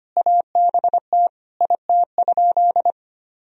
Morse code, heard as a keyed tone, about 25 wpm, 700 Hz. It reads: ABT IT?
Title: Sentences from Top 100 Words and Abbreviations 25wpm